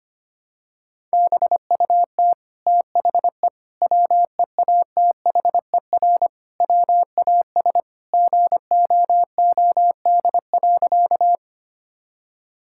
Morse code, heard as a keyed tone, about 25 wpm, 700 Hz. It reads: BUT THE WEATHER WAS GOOD.
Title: Sentences from Top 1000 Words - Review Part 5 - 25wpm